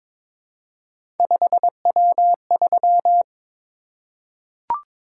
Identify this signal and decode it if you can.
Morse code, heard as a keyed tone, about 22 wpm, 700 Hz. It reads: 5W3
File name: practice / 22wpm / Three Letter-Number Combinations - 22wpm